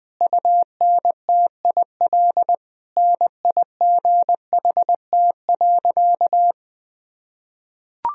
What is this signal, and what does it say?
Morse code, heard as a keyed tone, about 20 wpm, 700 Hz. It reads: UNTIL NIGHT.
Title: Sentences from Top 1000 Words - Review Part 2 - 20wpm